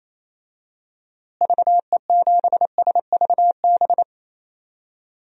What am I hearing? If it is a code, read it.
Morse code, heard as a keyed tone, about 28 wpm, 700 Hz. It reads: VE7SVB
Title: DXCC-100 Call Signs - 6 Characters - Repeated 2x - 28wpm